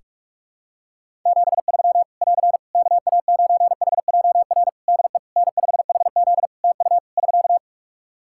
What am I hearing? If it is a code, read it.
Morse code, heard as a keyed tone, about 46 wpm, 700 Hz. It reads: 73 <AR> KA9HJR DE N5HZ TU <SK>
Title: Practice QSO - Exchange 4 - Ending - 45wpm